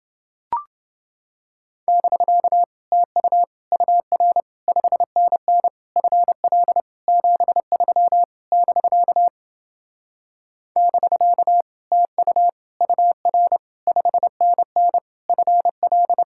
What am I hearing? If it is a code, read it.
Morse code, heard as a keyed tone, about 29 wpm, 700 Hz. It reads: <BK> TU UR 5NN FL 73 <BK> <BK> TU UR 5NN FL